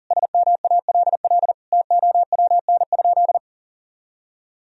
Morse code, heard as a keyed tone, about 40 wpm, 700 Hz. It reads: SMALL TOWN?
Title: Sentences from Top 1000 Words - Review Part 4 - 40wpm